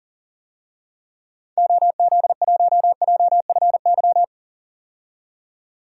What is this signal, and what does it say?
Morse code, heard as a keyed tone, about 37 wpm, 700 Hz. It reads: OZ1JFY
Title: DXCC-100 Call Signs - 6 Characters - 40wpm